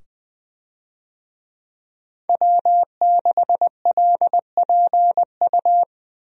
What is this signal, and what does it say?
Morse code, heard as a keyed tone, about 20 wpm, 700 Hz. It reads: W6LPU
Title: USA Call Signs - 5 Characters - 20wpm